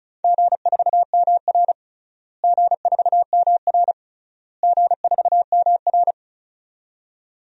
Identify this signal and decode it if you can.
Morse code, heard as a keyed tone, about 36 wpm, 700 Hz. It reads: G4MR G4MR G4MR
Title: DXCC-100 Call Signs - 4 Characters - Repeated 3x - 35wpm